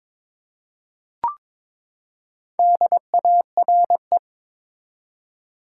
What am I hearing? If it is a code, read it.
Morse code, heard as a keyed tone, about 22 wpm, 700 Hz. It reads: DARE
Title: Four Letter Words - 22wpm